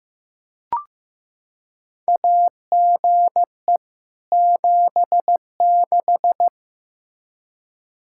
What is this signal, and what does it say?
Morse code, heard as a keyed tone, about 15 wpm, 700 Hz. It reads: AGE 76
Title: K1USN SST QSO Practice 15wpm